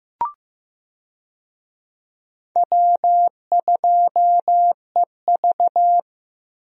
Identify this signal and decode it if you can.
Morse code, heard as a keyed tone, about 15 wpm, 700 Hz. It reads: W2EV
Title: USA Call Signs - 4 Characters - Repeated 2x - 15wpm